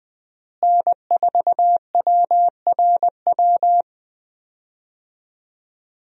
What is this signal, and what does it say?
Morse code, heard as a keyed tone, about 20 wpm, 700 Hz. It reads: N4WRW
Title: USA Call Signs - 5 Characters - Repeated 3x - 20wpm